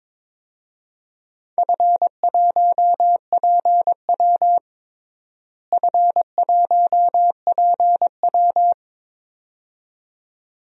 Morse code, heard as a keyed tone, about 22 wpm, 700 Hz. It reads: F1PW F1PW
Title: DXCC-100 Call Signs - 4 Characters - Repeated 2x - 22wpm